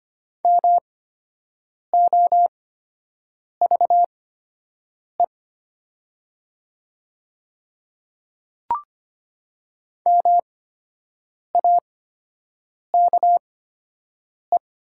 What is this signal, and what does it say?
Morse code, heard as a keyed tone, about 25 wpm, spaced as for 7 wpm, 700 Hz. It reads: MOVE MAKE
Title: 8x Character Spacing - Top 200 Words - 25wpm